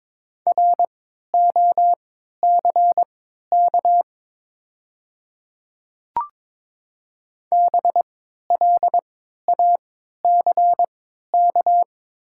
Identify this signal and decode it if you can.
Morse code, heard as a keyed tone, about 22 wpm, spaced as for 12 wpm, 690 Hz. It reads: ROCK BLACK
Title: 3x Character Spacing - Top 500 Words - 22wpm